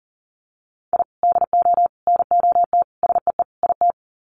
Morse code, heard as a keyed tone, about 40 wpm, 700 Hz. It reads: I DO NOT SEE IT
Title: Sets of 5 Words Encore2 40wpm